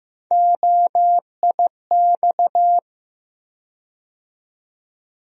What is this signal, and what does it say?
Morse code, heard as a keyed tone, about 15 wpm, 700 Hz. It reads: OIX
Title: USA Call Sign Postfixes - 3 Letters 15wpm